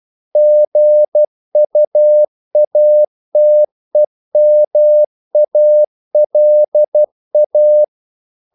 Morse code, heard as a keyed tone, about 12 wpm, 600 Hz. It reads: GUATEMALA
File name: CW / Practice / NationalCapitals12